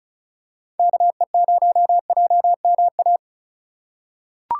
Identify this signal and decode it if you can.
Morse code, heard as a keyed tone, about 36 wpm, 700 Hz. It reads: KE0JMA
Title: USA Call Signs - 6 Characters - 2x Character Spacing - 35wpm